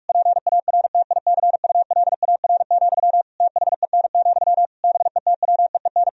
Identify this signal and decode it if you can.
Morse code, heard as a keyed tone, about 46 wpm, 700 Hz. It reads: PARTICULAR, THEN, BETWEEN
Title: Sentences from Top 1000 Words - Review Part 4 - 45wpm